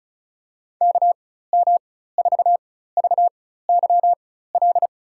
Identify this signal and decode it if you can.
Morse code, heard as a keyed tone, about 34 wpm, spaced as for 16 wpm, 700 Hz. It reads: KM4VYL
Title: USA Call Signs - 4x Character Spacing - 35wpm